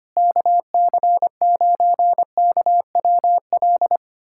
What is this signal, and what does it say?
Morse code, heard as a keyed tone, about 25 wpm, 700 Hz. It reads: KC9KWL
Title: USA Callsign DE Callsign - 6 Characters - 25wpm